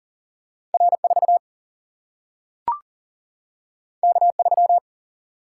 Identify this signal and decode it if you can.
Morse code, heard as a keyed tone, about 39 wpm, 700 Hz. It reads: R4 K3
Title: Single-Letter Single-Number 40wpm